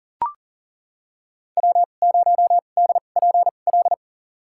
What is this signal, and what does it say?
Morse code, heard as a keyed tone, about 38 wpm, spaced as for 28 wpm, 700 Hz. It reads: W0DPL
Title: USA Call Signs - 5 Characters - 2x Character Spacing - 40wpm